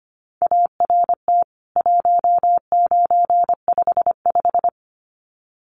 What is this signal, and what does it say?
Morse code, heard as a keyed tone, about 25 wpm, 700 Hz. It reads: ART 1955
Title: CWT Exchange 25wpm